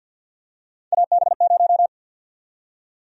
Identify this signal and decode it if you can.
Morse code, heard as a keyed tone, about 52 wpm, 700 Hz. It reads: AB0
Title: USA Call Sign Prefixes - 50wpm